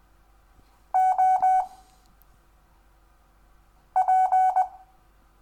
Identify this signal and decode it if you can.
Morse code, heard as a keyed tone, about 20 wpm, 750 Hz. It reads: O P
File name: cw / Lesson10 REVIEW